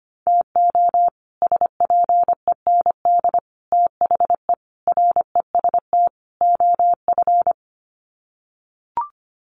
Morse code, heard as a keyed tone, about 25 wpm, 700 Hz. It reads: TO SPEND THE REST OF
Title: Sets of 5 Words Encore2 25wpm